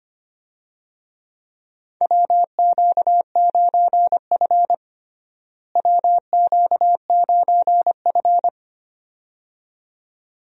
Morse code, heard as a keyed tone, about 25 wpm, 700 Hz. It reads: WQ9F WQ9F
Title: USA Call Signs - 4 Characters - Repeated 2x - 25wpm